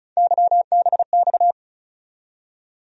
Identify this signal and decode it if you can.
Morse code, heard as a keyed tone, about 35 wpm, 700 Hz. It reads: YBX
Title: USA Call Sign Postfixes - 3 Letters 35wpm